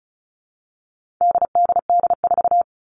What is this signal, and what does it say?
Morse code, heard as a keyed tone, about 34 wpm, 700 Hz. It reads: DDD4
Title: All Cut Numbers - 4-Digit 35wpm